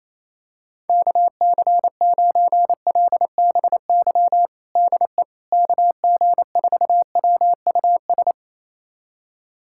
Morse code, heard as a keyed tone, about 28 wpm, 700 Hz. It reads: KC9LBY DE KG4WUS
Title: USA Callsign DE Callsign - 6 Characters - 28wpm